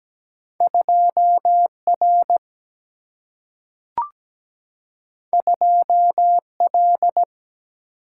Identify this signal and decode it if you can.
Morse code, heard as a keyed tone, about 17 wpm, 700 Hz. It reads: 2R 2L
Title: Single-Letter Single-Number 17wpm